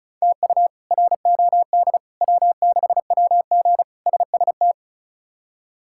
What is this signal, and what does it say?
Morse code, heard as a keyed tone, about 36 wpm, 700 Hz. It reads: TU ROD W6WG SST
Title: K1USN SST QSO Practice - 35wpm